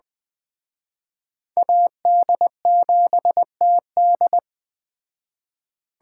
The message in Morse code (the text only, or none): AD7TD